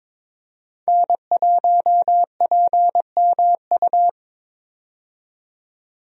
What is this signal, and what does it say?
Morse code, heard as a keyed tone, about 22 wpm, 700 Hz. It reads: N1PMU
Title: USA Call Signs - 5 Characters - 22wpm